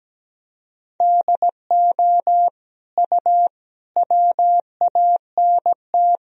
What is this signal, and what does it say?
Morse code, heard as a keyed tone, about 17 wpm, 700 Hz. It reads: DO U WANT
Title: Sentences from Top 300 Words and Abbreviations 17wpm